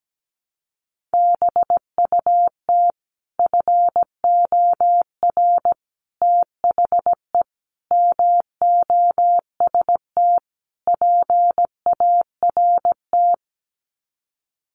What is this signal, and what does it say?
Morse code, heard as a keyed tone, about 17 wpm, 690 Hz. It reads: BUT FOR THE MOST PART
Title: Sets of 5 Words Encore 17wpm